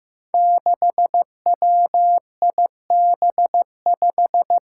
Morse code, heard as a keyed tone, about 15 wpm, 700 Hz. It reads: 6WIB5
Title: Five Challenging Letter-Number Combinations 15wpm